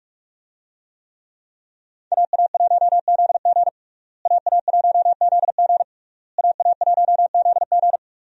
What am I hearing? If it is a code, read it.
Morse code, heard as a keyed tone, about 42 wpm, 700 Hz. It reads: AA1ZG AA1ZG AA1ZG
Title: USA Call Signs - 5 Characters - Repeated 3x - 45wpm